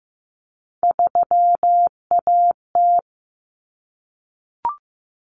Morse code, heard as a keyed tone, about 15 wpm, 700 Hz. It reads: 3AT
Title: Common Cut Numbers - 3-Digit 15wpm